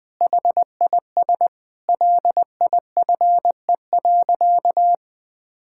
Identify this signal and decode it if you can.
Morse code, heard as a keyed tone, about 19 wpm, 700 Hz. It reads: HIS LIFE.
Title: Sentences with 6-Word Phrases - 20wpm